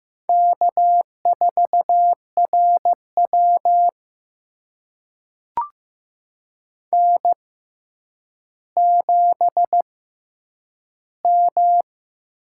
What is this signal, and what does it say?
Morse code, heard as a keyed tone, about 15 wpm, spaced as for 9 wpm, 700 Hz. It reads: K4RW N7M
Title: USA Call Signs - 6x Character Spacing - 15wpm